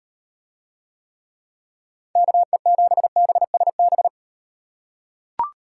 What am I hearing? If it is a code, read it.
Morse code, heard as a keyed tone, about 38 wpm, 700 Hz. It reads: KE7BSB